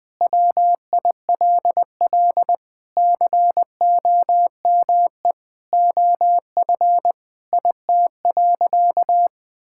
Morse code, heard as a keyed tone, about 21 wpm, 700 Hz. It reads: WILL COME OF IT.
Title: Sentences with 2-Word Phrases - 20wpm